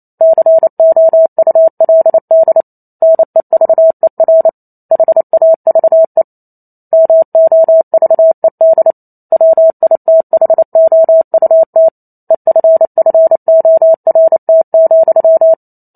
Morse code, heard as a keyed tone, about 28 wpm, 650 Hz. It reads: COULD NEVER HAVE MOVED WITHOUT EFFORT,